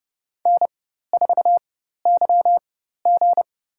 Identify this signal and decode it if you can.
Morse code, heard as a keyed tone, about 30 wpm, spaced as for 14 wpm, 700 Hz. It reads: N4YG
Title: USA Call Signs - 4 Characters - 4x Character Spacing - 30wpm